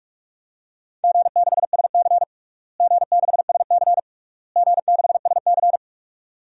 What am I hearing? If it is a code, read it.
Morse code, heard as a keyed tone, about 48 wpm, 700 Hz. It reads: G6SC G6SC G6SC
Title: DXCC-100 Call Signs - 4 Characters - Repeated 3x - 45wpm